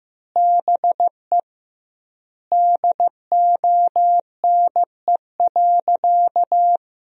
Morse code, heard as a keyed tone, about 15 wpm, 700 Hz. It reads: BE DONE.
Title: 2x Word-Spacing - Sentences from Top 500 Words - 15wpm